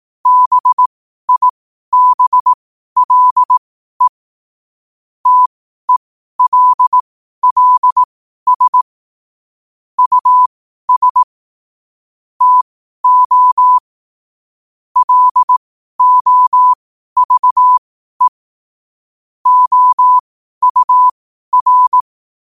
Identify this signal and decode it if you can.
Morse code, heard as a keyed tone, about 18 wpm, spaced as for 12 wpm, 1000 Hz. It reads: BIBLE TELLS US TO LOVE OUR
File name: qotdfiles / 12wpm